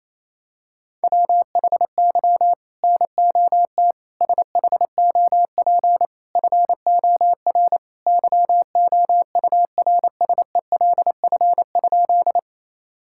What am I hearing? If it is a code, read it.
Morse code, heard as a keyed tone, about 28 wpm, 700 Hz. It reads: WHY NOT SHOP FOR YOURSELF?
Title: Sentences from Top 1600 Words 28wpm